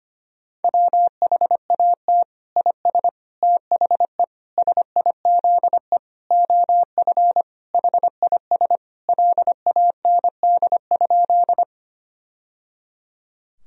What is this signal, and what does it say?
Morse code, heard as a keyed tone, about 25 wpm, 700 Hz. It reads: WHAT IS THE SIZE OF HIS LAND?
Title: Sentences from Top 1000 Words - Review Part 2 - 25wpm